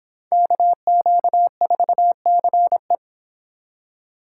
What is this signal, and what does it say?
Morse code, heard as a keyed tone, about 23 wpm, 700 Hz. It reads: KQ4CE